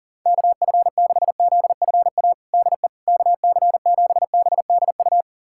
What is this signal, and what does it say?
Morse code, heard as a keyed tone, about 40 wpm, 700 Hz. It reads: KF6ZFA DE KC7BDU
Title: USA Callsign DE Callsign - 6 Characters - 40wpm